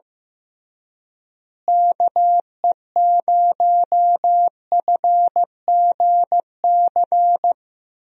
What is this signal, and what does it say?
Morse code, heard as a keyed tone, about 16 wpm, 700 Hz. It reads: KE0FGC
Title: USA Call Signs - 6 Characters - Repeated 3x - 15wpm